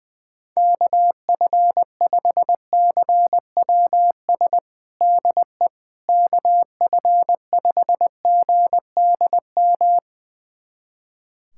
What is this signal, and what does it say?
Morse code, heard as a keyed tone, about 20 wpm, 700 Hz. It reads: KF5CWS DE KF5GDM
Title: USA Callsign DE Callsign - 6 Characters - 20wpm